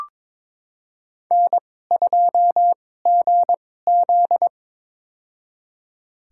Morse code, heard as a keyed tone, about 22 wpm, spaced as for 16 wpm, 700 Hz. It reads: N2GZ
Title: USA Call Signs - 4 Characters - 2x Character Spacing - 22wpm